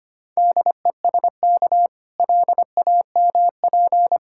Morse code, heard as a keyed tone, about 27 wpm, 700 Hz. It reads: DESK LAMP
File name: Easy Words - Speed-Racing with Hints - 25wpm